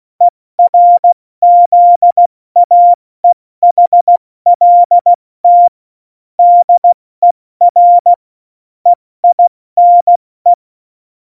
Morse code, heard as a keyed tone, about 16 wpm, spaced as for 14 wpm, 700 Hz. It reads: ERZAEHLT DER EINE
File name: Morsekurs / Witze / Witz 08. Januar 2019, 70 F80